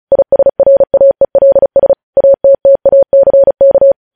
Morse code, heard as a keyed tone, about 36 wpm, 550 Hz. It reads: ISRAELS ATTACK
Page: news headlines in morse code at 35 wpm